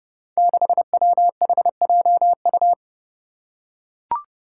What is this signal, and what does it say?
Morse code, heard as a keyed tone, about 30 wpm, 700 Hz. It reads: 6WHJU
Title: Five Challenging Letter-Number Combinations 30wpm